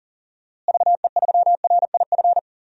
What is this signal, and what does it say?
Morse code, heard as a keyed tone, about 39 wpm, 700 Hz. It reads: VE3RIF
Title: DXCC-100 Call Signs - 6 Characters - Repeated 3x - 40wpm